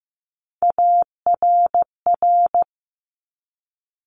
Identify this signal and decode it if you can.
Morse code, heard as a keyed tone, about 15 wpm, 690 Hz. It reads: ARR
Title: Top Three Letter Combinations in Words 15wpm